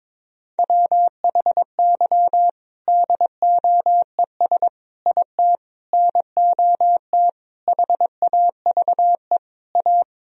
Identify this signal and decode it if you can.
Morse code, heard as a keyed tone, about 21 wpm, 700 Hz. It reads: WHY DOES IT NOT HAVE A
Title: Sentences from Top 1000 Words - Review Part 3 - 22wpm